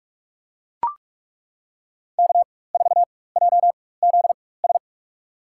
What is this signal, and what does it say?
Morse code, heard as a keyed tone, about 42 wpm, spaced as for 20 wpm, 700 Hz. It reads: K4JZS
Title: USA Call Signs - 5 Characters - 4x Character Spacing - 45wpm